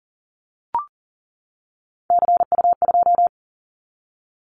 Morse code, heard as a keyed tone, about 39 wpm, 700 Hz. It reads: CU2
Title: Callsign Prefixes for Top 100 DXCC - 40wpm